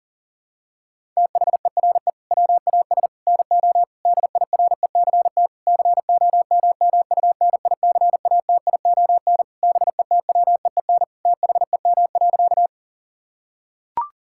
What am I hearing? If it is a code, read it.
Morse code, heard as a keyed tone, about 40 wpm, 700 Hz. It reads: THERE WAS NO DIRECT COMMUNICATION BETWEEN THEM.
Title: Sentences from Top 1000 Words - Review Part 6 - 40wpm